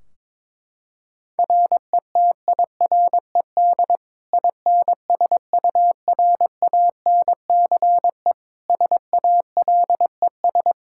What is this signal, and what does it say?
Morse code, heard as a keyed tone, about 22 wpm, 700 Hz. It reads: RETIRED INSURANCE SALES
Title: Practice QSO - Exchange 3 - More Info - 22wpm